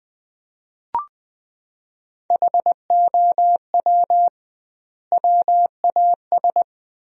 Morse code, heard as a keyed tone, about 20 wpm, 700 Hz. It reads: HOW WAS
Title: 2x Word-Spacing - Sentences from Top 100 Words - 20wpm